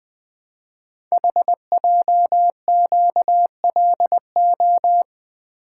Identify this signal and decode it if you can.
Morse code, heard as a keyed tone, about 20 wpm, 700 Hz. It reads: HJQLO
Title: Code Groups - 5 Letters - 20wpm